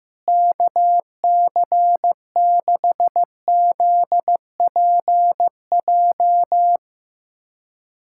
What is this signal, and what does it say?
Morse code, heard as a keyed tone, about 15 wpm, 700 Hz. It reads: KC6ZPJ
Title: USA Callsign DE Callsign - 6 Characters - 15wpm